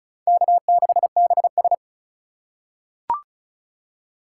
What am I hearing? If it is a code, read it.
Morse code, heard as a keyed tone, about 36 wpm, 700 Hz. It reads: K6BS
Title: USA Call Signs - 4 Characters - 4x Character Spacing - 35wpm